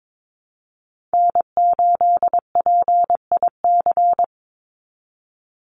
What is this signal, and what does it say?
Morse code, heard as a keyed tone, about 22 wpm, 700 Hz. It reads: N8PIC
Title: USA Call Signs 22wpm